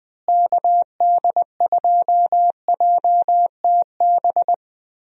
Morse code, heard as a keyed tone, about 19 wpm, 700 Hz. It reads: KD2JTB
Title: USA Call Signs - 6 Characters - Repeated 3x - 20wpm